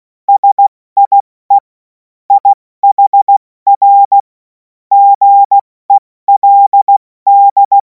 Morse code, heard as a keyed tone, about 16 wpm, 800 Hz. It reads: SIE IHR GELD
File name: Morsekurs / Witze / Witz 03. Februar 2026 70 F80